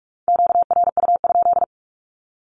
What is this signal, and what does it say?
Morse code, heard as a keyed tone, about 44 wpm, 700 Hz. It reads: QRU?
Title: Q-Codes Repeated 3x 45wpm